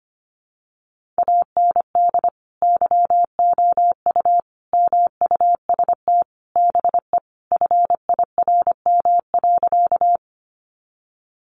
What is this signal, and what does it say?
Morse code, heard as a keyed tone, about 25 wpm, 700 Hz. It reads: AND YOU MUST BE FIRM.
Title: Sentences from Top 1400 Words 25wpm